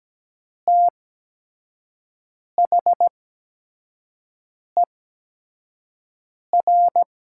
Morse code, heard as a keyed tone, about 17 wpm, spaced as for 5 wpm, 700 Hz. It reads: THER
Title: 8x Character Spacing - Top 100 Words - 17wpm